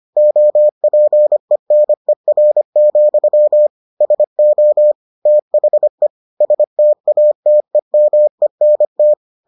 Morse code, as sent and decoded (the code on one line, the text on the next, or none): --- .--. -. .-. --..-- / ... --- / - .... / ... - .- - . -- . -. -
OPENER, SO THE STATEMENT